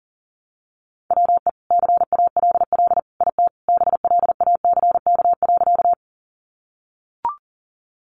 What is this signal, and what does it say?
Morse code, heard as a keyed tone, about 40 wpm, 700 Hz. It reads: WE CALL IT BLACK.
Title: Sentences from Top 400 Words 40wpm